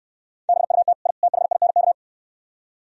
Morse code, heard as a keyed tone, about 68 wpm, 700 Hz. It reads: BUT I THINK